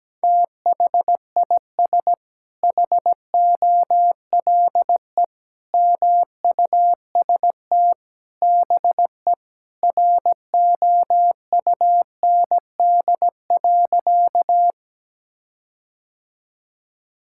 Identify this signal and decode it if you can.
Morse code, heard as a keyed tone, about 17 wpm, 700 Hz. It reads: THIS HOLE MUST BE ROUND.
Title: Sentences from Top 1600 Words 17wpm